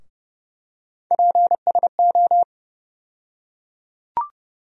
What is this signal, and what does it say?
Morse code, heard as a keyed tone, about 32 wpm, 700 Hz. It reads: PSO
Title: USA Call Sign Postfixes - 3 Letters 30wpm